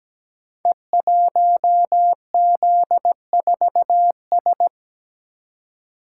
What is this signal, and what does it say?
Morse code, heard as a keyed tone, about 17 wpm, 700 Hz. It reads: E1Z4S
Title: Code Groups - 5 Letter:Numbers - 17wpm